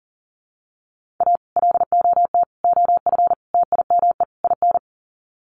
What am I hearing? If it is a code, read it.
Morse code, heard as a keyed tone, about 40 wpm, 700 Hz. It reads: A LOT OF TIME IN
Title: Sets of 5 Words Encore2 40wpm